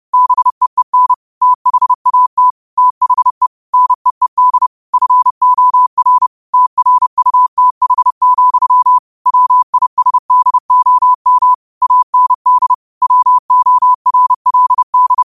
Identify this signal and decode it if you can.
Morse code, heard as a keyed tone, about 30 wpm, 1000 Hz. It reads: BEEN THAT THE NEED FOR TRUTH, WISDOM AND WORLD
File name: qotdfiles / 30wpm